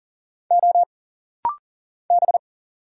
Morse code, heard as a keyed tone, about 40 wpm, 700 Hz. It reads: O B
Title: Single Letter-Number - Rapid-Fire 40wpm